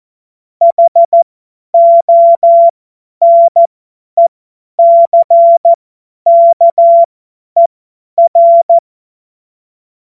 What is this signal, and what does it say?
Morse code, heard as a keyed tone, about 14 wpm, spaced as for 10 wpm, 680 Hz. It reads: HONECKER
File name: Morsekurs / Witze / Witz 20. Juli 2021, 50 F70